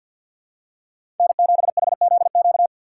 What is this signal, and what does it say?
Morse code, heard as a keyed tone, about 52 wpm, 700 Hz. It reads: N7HZQ
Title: USA Call Signs - 5 Characters - 2x Character Spacing - 50wpm